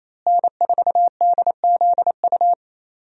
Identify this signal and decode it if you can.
Morse code, heard as a keyed tone, about 28 wpm, 700 Hz. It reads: N4DZU
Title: USA Call Signs - 5 Characters - Repeated 2x - 28wpm